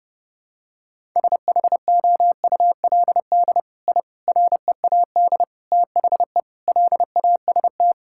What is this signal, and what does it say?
Morse code, heard as a keyed tone, about 30 wpm, 700 Hz. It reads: SHOULD I READ THE LAST
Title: Sentences from Top 1000 Words - Review Part 5 - 30wpm